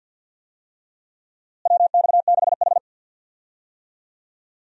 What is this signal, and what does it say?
Morse code, heard as a keyed tone, about 48 wpm, 700 Hz. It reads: WX6H
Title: USA Call Signs - 4 Characters - Repeated 2x - 50wpm